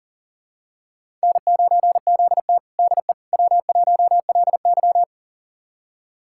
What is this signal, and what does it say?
Morse code, heard as a keyed tone, about 40 wpm, 700 Hz. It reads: N9ZT DE W1LY
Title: USA Callsign DE Callsign - 4 Characters - 40wpm